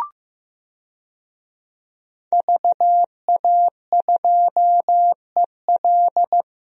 Morse code, heard as a keyed tone, about 16 wpm, 700 Hz. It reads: VA2EL
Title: DXCC-100 Call Signs - 5 Characters - Repeated 2x - 15wpm